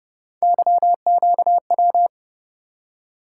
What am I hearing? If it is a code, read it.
Morse code, heard as a keyed tone, about 30 wpm, 700 Hz. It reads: YQW